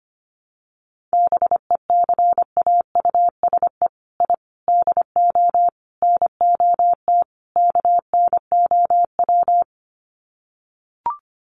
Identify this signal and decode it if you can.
Morse code, heard as a keyed tone, about 25 wpm, 700 Hz. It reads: BECAUSE I DO NOT KNOW
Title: Sets of 5 Words Encore2 25wpm